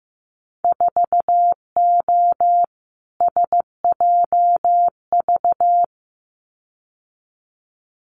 Morse code, heard as a keyed tone, about 15 wpm, 700 Hz. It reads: 4O SJV
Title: Winter Field Day Exchanges Encore 15wpm